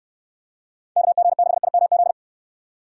Morse code, heard as a keyed tone, about 67 wpm, 700 Hz. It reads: KG6IMZ